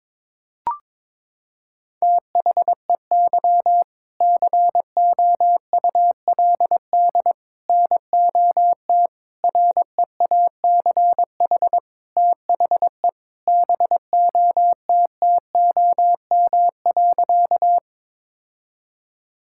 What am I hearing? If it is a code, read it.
Morse code, heard as a keyed tone, about 22 wpm, 700 Hz. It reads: THEY COULD NOT REACH THE BOTTOM.
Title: Sentences from Top 1000 Words - Review Part 6 - 22wpm